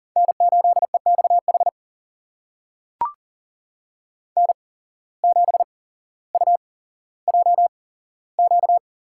Morse code, heard as a keyed tone, about 40 wpm, spaced as for 17 wpm, 700 Hz. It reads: N8EXH N7UJQ
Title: USA Call Signs - 5 Characters - 8x Character Spacing - 40wpm